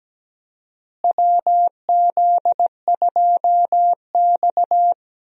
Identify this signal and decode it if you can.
Morse code, heard as a keyed tone, about 17 wpm, 700 Hz. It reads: WZ2X
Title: USA Call Signs - 4 Characters - Repeated 2x - 17wpm